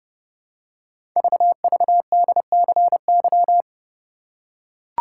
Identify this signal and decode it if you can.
Morse code, heard as a keyed tone, about 28 wpm, 700 Hz. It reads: VVDCY
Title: Code Groups - 5 Letters - 30wpm